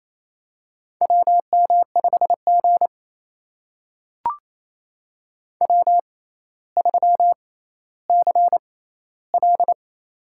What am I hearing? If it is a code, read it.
Morse code, heard as a keyed tone, about 28 wpm, spaced as for 14 wpm, 700 Hz. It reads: WM5G W3CL
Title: USA Call Signs - 4 Characters - 6x Character Spacing - 28wpm